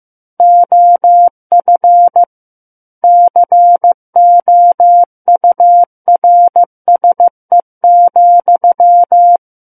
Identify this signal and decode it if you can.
Morse code, heard as a keyed tone, about 15 wpm, 700 Hz. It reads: OF COURSE,
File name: Book One / 06-Chapter 6 - 15 WPM